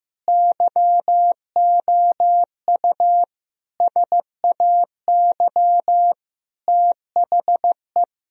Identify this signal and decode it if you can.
Morse code, heard as a keyed tone, about 15 wpm, 700 Hz. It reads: YOU SAY THE
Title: Sentences from Top 1000 Words - Review Part 3 - 15wpm